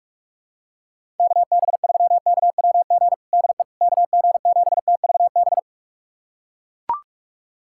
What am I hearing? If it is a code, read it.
Morse code, heard as a keyed tone, about 42 wpm, 700 Hz. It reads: KB3KWG DE KG7TVB
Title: USA Callsign DE Callsign 45wpm